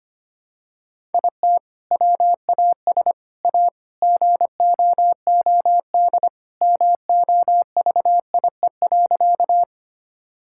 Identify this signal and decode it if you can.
Morse code, heard as a keyed tone, about 25 wpm, 700 Hz. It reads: IT WAS A GOOD MOVIE.
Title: Sentences from Top 1000 Words - Review Part 5 - 25wpm